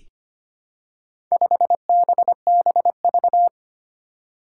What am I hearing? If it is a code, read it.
Morse code, heard as a keyed tone, about 25 wpm, 700 Hz. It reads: 5BBV
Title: Four Challenging Letter-Number Combinations 25wpm